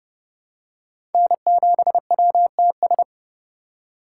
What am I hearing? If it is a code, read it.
Morse code, heard as a keyed tone, about 32 wpm, 700 Hz. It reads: N7WTS
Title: USA Call Signs - 5 Characters - 3x Character Spacing - 30wpm